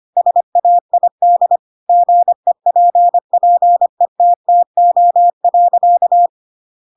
Morse code, heard as a keyed tone, about 25 wpm, 700 Hz. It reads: SAID GEPPETTO.